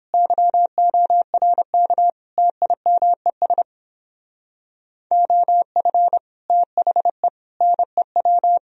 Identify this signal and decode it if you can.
Morse code, heard as a keyed tone, about 26 wpm, 700 Hz. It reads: YORK TIMES OF THE NEW
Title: Speed-Racing - Sets of 5 Words 22wpm